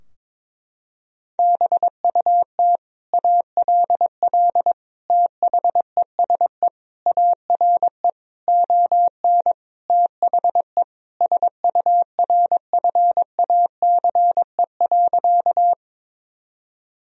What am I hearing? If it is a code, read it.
Morse code, heard as a keyed tone, about 22 wpm, 700 Hz. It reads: BUT ALL THESE ARE ON THE SURFACE.
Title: Sentences from Top 1000 Words - Review Part 3 - 22wpm